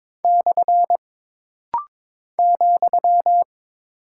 Morse code, heard as a keyed tone, about 22 wpm, 700 Hz. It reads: / ,
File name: practice / 22wpm / Single Punctuation-Prosign - Rapid-Fire - 22wpm